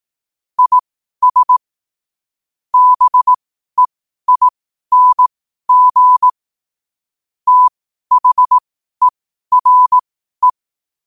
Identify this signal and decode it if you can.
Morse code, heard as a keyed tone, about 18 wpm, spaced as for 12 wpm, 1000 Hz. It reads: IS BEING THERE
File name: qotdfiles / 12wpm